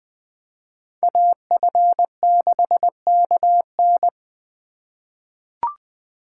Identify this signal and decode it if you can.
Morse code, heard as a keyed tone, about 20 wpm, 700 Hz. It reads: AF6KN
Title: USA Call Signs - 5 Characters - 2x Character Spacing - 20wpm